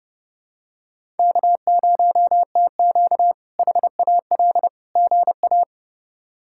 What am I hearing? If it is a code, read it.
Morse code, heard as a keyed tone, about 30 wpm, 700 Hz. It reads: K0TQ HAL GA
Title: K1USN SST QSO Practice - 30wpm